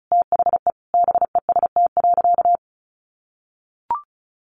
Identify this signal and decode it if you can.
Morse code, heard as a keyed tone, about 34 wpm, 700 Hz. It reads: THE BEST.
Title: Sentences from Top 600 Words 35wpm